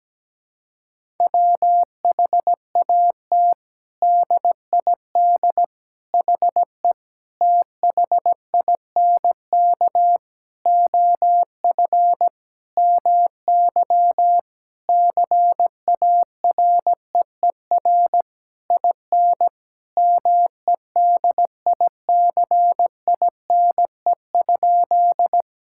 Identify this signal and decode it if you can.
Morse code, heard as a keyed tone, about 17 wpm, 700 Hz. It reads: WHAT DID HE THINK OF MY CAREER IN MEDICINE?
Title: Sentences from Top 1000 Words - Review Part 6 - 17wpm